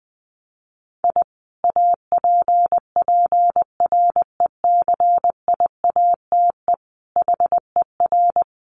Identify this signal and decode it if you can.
Morse code, heard as a keyed tone, about 19 wpm, 700 Hz. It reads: I APPRECIATE HER
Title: Sentences from Top 1300 Words 20wpm